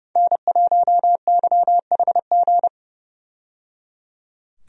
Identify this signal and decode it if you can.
Morse code, heard as a keyed tone, about 30 wpm, 700 Hz. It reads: N1YHG
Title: USA Call Signs - 5 Characters - 30wpm